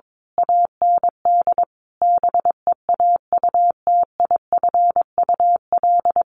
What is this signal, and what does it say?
Morse code, heard as a keyed tone, about 22 wpm, 700 Hz. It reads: AND BEAUTIFUL
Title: Sentences from Top 1100 Words 22wpm